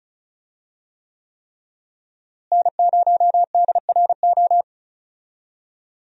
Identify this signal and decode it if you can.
Morse code, heard as a keyed tone, about 34 wpm, 700 Hz. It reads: N0DRO
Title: USA Call Signs - 5 Characters - Repeated 3x - 35wpm